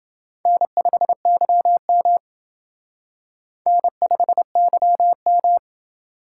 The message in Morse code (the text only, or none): N5YM N5YM